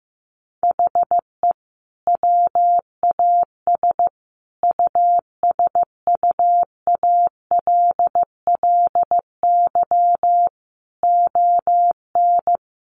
Morse code, heard as a keyed tone, about 15 wpm, 700 Hz. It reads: HE WAS USUALLY ON
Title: Sentences from Top 1400 Words 15wpm